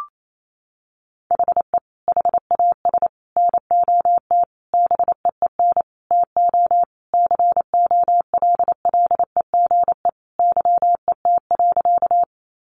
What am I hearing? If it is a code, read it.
Morse code, heard as a keyed tone, about 28 wpm, 700 Hz. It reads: HE HAS NOT BEEN TO COLLEGE YET.
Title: Sentences from Top 1300 Words 28wpm